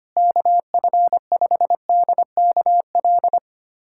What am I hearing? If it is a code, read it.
Morse code, heard as a keyed tone, about 25 wpm, 700 Hz. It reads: KF5DKL
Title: USA Callsign DE Callsign - 6 Characters - 25wpm